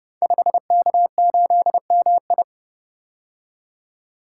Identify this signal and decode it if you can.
Morse code, heard as a keyed tone, about 30 wpm, 700 Hz. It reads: 5K8MI
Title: Code Groups - 5 Letter:Numbers - 30wpm